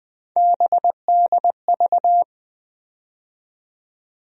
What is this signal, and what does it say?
Morse code, heard as a keyed tone, about 20 wpm, 700 Hz. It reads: BDV
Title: USA Call Sign Postfixes - 3 Letters 20wpm